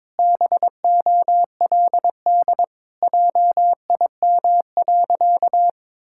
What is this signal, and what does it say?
Morse code, heard as a keyed tone, about 22 wpm, 700 Hz. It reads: BOLD JIM.
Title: Pangram Challenge v2 - 22wpm